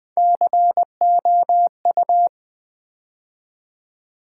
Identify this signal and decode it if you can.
Morse code, heard as a keyed tone, about 20 wpm, 700 Hz. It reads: COU
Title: Build A Word - 3 Letters At A Time - 6-Letter Words - 20wpm